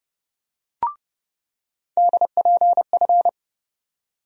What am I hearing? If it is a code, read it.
Morse code, heard as a keyed tone, about 30 wpm, 700 Hz. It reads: DPF
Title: USA Call Sign Postfixes - 3 Letters 30wpm